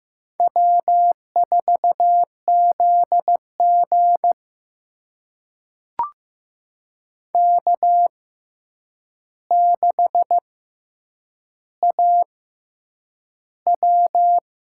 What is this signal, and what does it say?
Morse code, heard as a keyed tone, about 15 wpm, spaced as for 8 wpm, 700 Hz. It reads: W4ZG K6AW
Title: USA Call Signs - 4 Characters - 6x Character Spacing - 15wpm